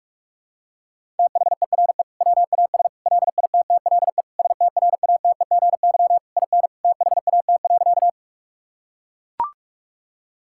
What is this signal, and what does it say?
Morse code, heard as a keyed tone, about 41 wpm, 700 Hz. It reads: THERE WAS LITTLE STRATEGY IN THAT.
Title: Sentences from Top 1000 Words - Review Part 5 - 45wpm